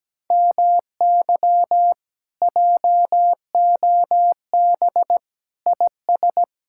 Morse code, heard as a keyed tone, about 17 wpm, 700 Hz. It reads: MY JOB IS
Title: Speed-Racing Common QSO Phrases - 17wpm